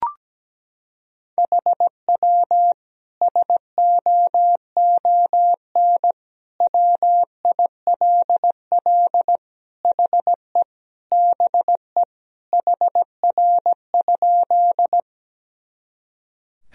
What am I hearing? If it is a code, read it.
Morse code, heard as a keyed tone, about 17 wpm, 700 Hz. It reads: HW SOON WILL HE BE HR?
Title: Sentences from Top 400 Words and Abbreviations 17wpm